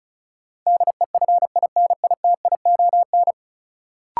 Morse code, heard as a keyed tone, about 36 wpm, 700 Hz. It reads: DEFINITION
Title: Sentences from Top 1000 Words - Review Part 4 - New Word Drill - 35wpm